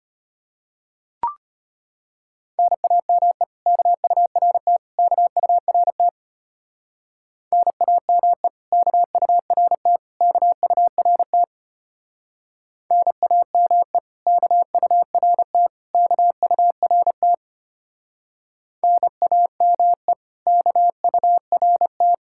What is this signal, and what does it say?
Morse code, heard as a keyed tone, about 30 wpm, 700 Hz. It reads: NAME KURT KURT NAME KURT KURT NAME KURT KURT NAME KURT
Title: Speed-Racing Common QSO Phrases Encore - 25wpm